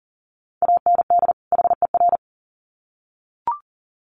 Morse code, heard as a keyed tone, about 40 wpm, 700 Hz. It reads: AND HER
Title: Sets of 2 Words 40wpm